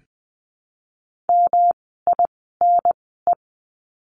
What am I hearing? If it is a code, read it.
Morse code, heard as a keyed tone, about 20 wpm, spaced as for 14 wpm, 700 Hz. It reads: MINE
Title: 108 Lesson 18f - Letter m - Words 20wpm